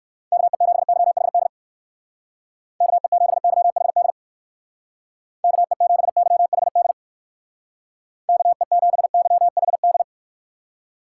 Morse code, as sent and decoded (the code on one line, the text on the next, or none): -.- . --... -.-- .... -.. / -.- . --... -.-- .... -.. / -.- . --... -.-- .... -.. / -.- . --... -.-- .... -..
KE7YHD KE7YHD KE7YHD KE7YHD